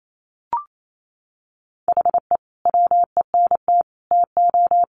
Morse code, heard as a keyed tone, about 28 wpm, 700 Hz. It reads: HE WENT TO
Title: Sentences from Top 200 Words 28wpm